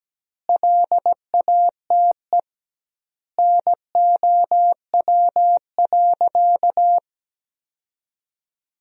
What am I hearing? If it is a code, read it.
Morse code, heard as a keyed tone, about 17 wpm, 700 Hz. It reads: LATE NOW.
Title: 2x Word-Spacing - Sentences from Top 300 Words - 17wpm